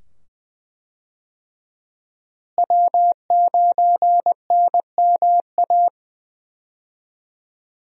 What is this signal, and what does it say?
Morse code, heard as a keyed tone, about 20 wpm, 700 Hz. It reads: W9NMA
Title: USA Call Signs - 5 Characters - Repeated 2x - 20wpm